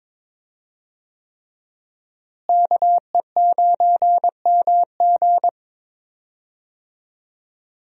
Morse code, heard as a keyed tone, about 22 wpm, 700 Hz. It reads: KE9MG
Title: USA Call Signs - 5 Characters - Repeated 3x - 22wpm